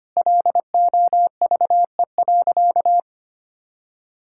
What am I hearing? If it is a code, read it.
Morse code, heard as a keyed tone, about 25 wpm, 700 Hz. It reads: LOVE.
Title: Sentences from Top 1600 Words 25wpm